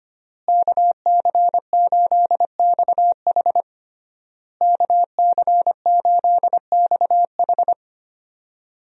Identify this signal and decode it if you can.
Morse code, heard as a keyed tone, about 25 wpm, 700 Hz. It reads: KC8XH KC8XH
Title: USA Call Signs - 5 Characters - Repeated 3x - 25wpm